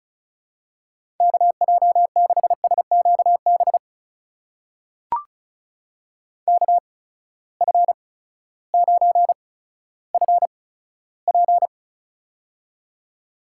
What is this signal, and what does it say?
Morse code, heard as a keyed tone, about 34 wpm, spaced as for 16 wpm, 700 Hz. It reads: KJ6SQB KF9FP
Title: USA Call Signs - 8x Character Spacing - 35wpm